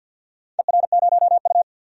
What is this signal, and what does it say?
Morse code, heard as a keyed tone, about 46 wpm, 700 Hz. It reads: ER0U
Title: DXCC-100 Call Signs - 4 Characters - 50wpm